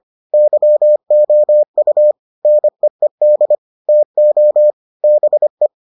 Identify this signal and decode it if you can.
Morse code, heard as a keyed tone, about 25 wpm, 600 Hz. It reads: YOU NEED TO BE